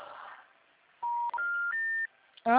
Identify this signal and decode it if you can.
Special information tone.